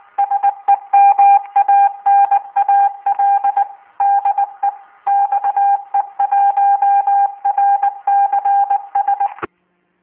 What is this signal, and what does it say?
Morse code, heard as a keyed tone, about 20 wpm, 800 Hz. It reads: SEMANAL DE XE1RCS